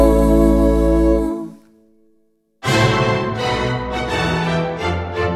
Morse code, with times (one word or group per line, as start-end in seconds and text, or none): none